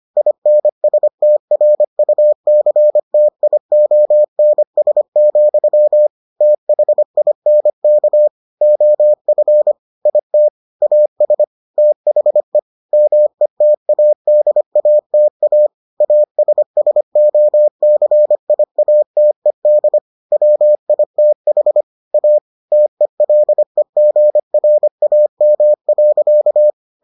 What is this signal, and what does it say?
Morse code, heard as a keyed tone, about 25 wpm, 600 Hz. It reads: INSTRUCTIONS, THINK OF IT AS THE METADATA ASSOCIATED WITH A TELEGRAM.